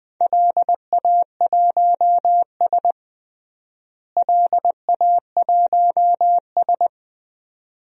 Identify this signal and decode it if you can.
Morse code, heard as a keyed tone, about 20 wpm, 700 Hz. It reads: LA1S LA1S
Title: DXCC-100 Call Signs - 4 Characters - Repeated 2x - 20wpm